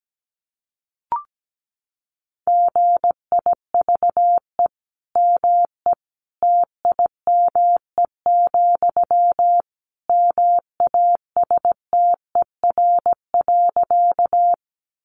Morse code, heard as a keyed tone, about 17 wpm, 700 Hz. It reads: GIVE ME TIME, MASTER.
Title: Sentences from Top 1300 Words 17wpm